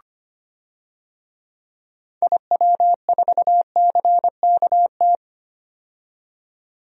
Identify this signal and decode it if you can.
Morse code, heard as a keyed tone, about 25 wpm, 700 Hz. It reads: IW4CKT
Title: DXCC-100 Call Signs - 6 Characters - 25wpm